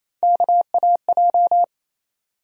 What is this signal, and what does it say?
Morse code, heard as a keyed tone, about 28 wpm, 700 Hz. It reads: KAJ
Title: USA Call Sign Postfixes - 3 Letters 28wpm